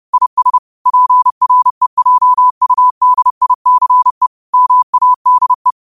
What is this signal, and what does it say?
Morse code, heard as a keyed tone, about 30 wpm, 1000 Hz. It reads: IS PREJUDICE MADE
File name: qotdfiles / 30wpm